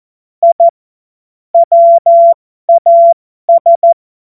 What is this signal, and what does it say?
Morse code, heard as a keyed tone, about 14 wpm, spaced as for 12 wpm, 680 Hz. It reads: I WAS
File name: Morsekurs / Witze